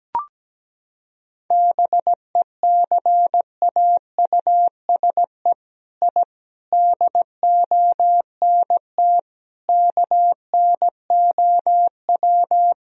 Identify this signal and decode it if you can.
Morse code, heard as a keyed tone, about 17 wpm, 700 Hz. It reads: BECAUSE I DONT KNOW
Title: Sentences from Top 1000 Words - Review Part 3 - 17wpm